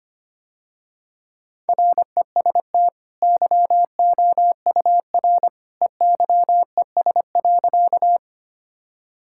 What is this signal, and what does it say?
Morse code, heard as a keyed tone, about 25 wpm, 700 Hz. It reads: REST YOUR EYES.
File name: Easy Sentences with Hints - 25wpm